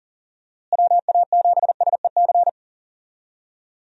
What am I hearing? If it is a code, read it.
Morse code, heard as a keyed tone, about 38 wpm, 700 Hz. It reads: WA7SEC
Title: USA Call Signs - 6 Characters - Repeated 3x - 40wpm